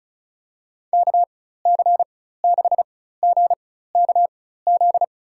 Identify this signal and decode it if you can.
Morse code, heard as a keyed tone, about 36 wpm, spaced as for 16 wpm, 700 Hz. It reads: KC6GKZ
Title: USA Call Signs - 6 Characters - 4x Character Spacing - 35wpm